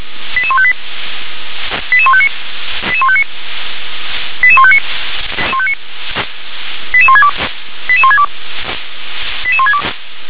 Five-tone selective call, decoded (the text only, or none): 88158 88159 88159 88159 88162 88162 88162